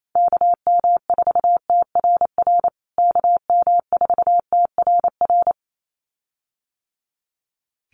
Morse code, heard as a keyed tone, about 28 wpm, 700 Hz. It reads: KM4TRR KM4TRR
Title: USA Call Signs Repeated 2x 28wpm